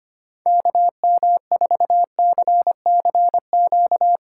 Morse code, heard as a keyed tone, about 25 wpm, 700 Hz. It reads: KM4CCQ
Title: Speed Racing - USA 2x3 Call Signs 25wpm